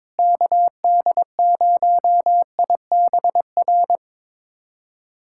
Morse code, heard as a keyed tone, about 22 wpm, 700 Hz. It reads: KD0IBR
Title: USA Callsign DE Callsign 22wpm